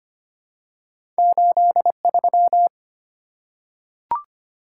Morse code, heard as a keyed tone, about 25 wpm, 700 Hz. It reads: 83
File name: Two Letter-Number Combinations - 25wpm